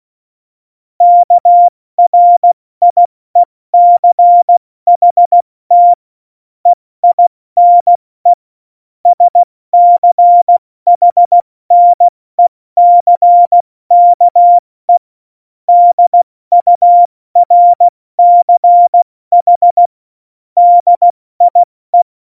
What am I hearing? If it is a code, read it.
Morse code, heard as a keyed tone, about 16 wpm, spaced as for 14 wpm, 700 Hz. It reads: KRIECHT EINE SCHNECKE DURCH DIE